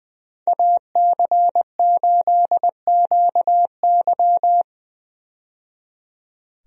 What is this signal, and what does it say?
Morse code, heard as a keyed tone, about 20 wpm, 700 Hz. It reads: AC8QY
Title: USA Call Signs - 5 Characters - 20wpm